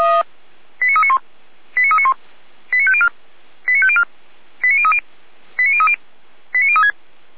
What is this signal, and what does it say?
Five-tone selective call (ZVEI), heard as quick runs of five tones, tones of about 70 ms. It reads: 89391 89391 89504 89504 89030 89030 89036